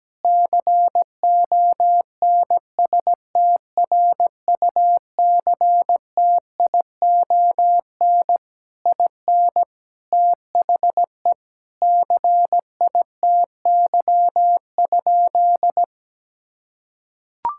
Morse code, heard as a keyed tone, about 17 wpm, 700 Hz. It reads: CONSTRUCTION IN THE CITY?
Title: Sentences from Top 1600 Words 17wpm